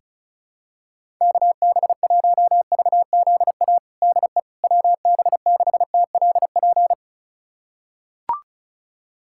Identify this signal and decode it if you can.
Morse code, heard as a keyed tone, about 36 wpm, 700 Hz. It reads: KB1VZA DE WB6TLP
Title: USA Callsign DE Callsign - 6 Characters - 35wpm